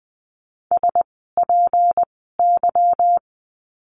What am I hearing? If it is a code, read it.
Morse code, heard as a keyed tone, about 20 wpm, spaced as for 15 wpm, 700 Hz. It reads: SPY
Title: 153 Lesson 25f - Letter p - Words 20wpm